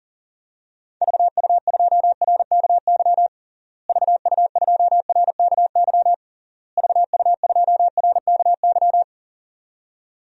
Morse code, heard as a keyed tone, about 40 wpm, 700 Hz. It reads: VU2RKY VU2RKY VU2RKY
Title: DXCC-100 Call Signs - 6 Characters - Repeated 3x - 40wpm